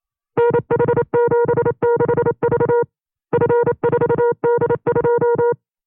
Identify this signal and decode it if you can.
Morse code, heard as a keyed tone, about 28 wpm, 480 Hz. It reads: NH76V F4D2
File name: FreeFiles / CW-practice / 25WPM-side1